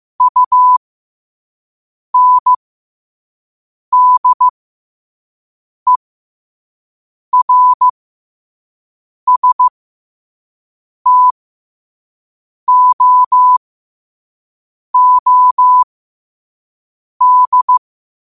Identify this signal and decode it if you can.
Morse code, heard as a keyed tone, about 15 wpm, spaced as for 5 wpm, 1000 Hz. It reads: UNDERSTOOD
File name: qotdfiles / 5wpm